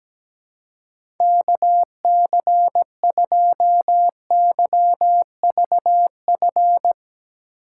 Morse code, heard as a keyed tone, about 17 wpm, 700 Hz. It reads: KC2YVF
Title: USA Call Signs - 6 Characters - 4x Character Spacing - 17wpm